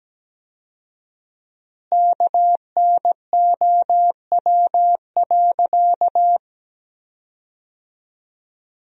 Morse code, heard as a keyed tone, about 17 wpm, 700 Hz. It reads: KNOW.
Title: 4x Word-Spacing - Sentences from Top 100 Words - 17wpm